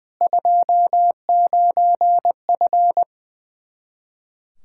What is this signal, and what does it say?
Morse code, heard as a keyed tone, about 20 wpm, 700 Hz. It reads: 29F
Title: Practice QSO - Exchange 2 - Info - 20wpm